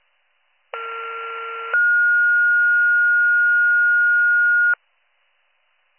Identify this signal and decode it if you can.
Two-tone page, heard as a one-second tone, then a three-second tone, about 1370 then 1430 Hz. It is heard from 0.7 to 4.7 s.